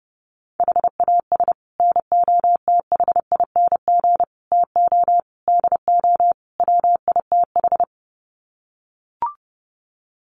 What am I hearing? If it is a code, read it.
Morse code, heard as a keyed tone, about 30 wpm, 700 Hz. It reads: HAS NOTHING TO DO WITH
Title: Sets of 5 Words 30wpm